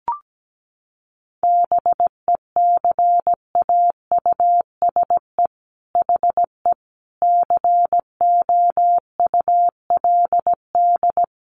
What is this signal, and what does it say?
Morse code, heard as a keyed tone, about 17 wpm, 700 Hz. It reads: BECAUSE HE COULD
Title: Sentences from Top 1400 Words 17wpm